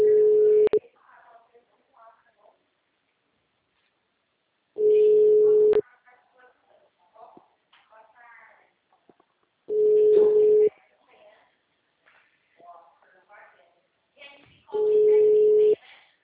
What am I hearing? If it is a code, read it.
Ringback tone (Europe, 425 Hz), with speech.